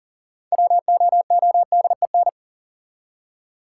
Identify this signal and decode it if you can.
Morse code, heard as a keyed tone, about 40 wpm, 700 Hz. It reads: WOODEN
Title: Six Letter Words - 40wpm